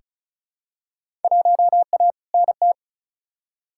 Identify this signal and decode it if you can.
Morse code, heard as a keyed tone, about 36 wpm, 700 Hz. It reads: 1A NT
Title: Field Day Exchanges Encore - 35wpm